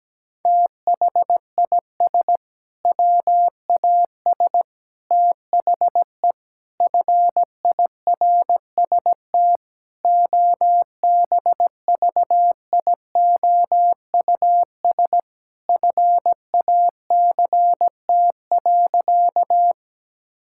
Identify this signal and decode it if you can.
Morse code, heard as a keyed tone, about 17 wpm, 700 Hz. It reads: THIS WAS THE FIRST OBVIOUS FACT.
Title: Sentences from Top 1600 Words 17wpm